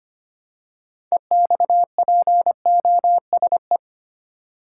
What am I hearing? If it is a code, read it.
Morse code, heard as a keyed tone, about 25 wpm, 700 Hz. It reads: EXPOSE
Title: Six Letter Words - 25wpm